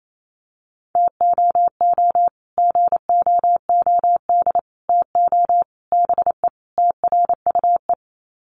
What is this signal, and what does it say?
Morse code, heard as a keyed tone, about 28 wpm, 700 Hz. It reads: TOO GOOD TO BE TRUE
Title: Sets of 5 Words Encore 28wpm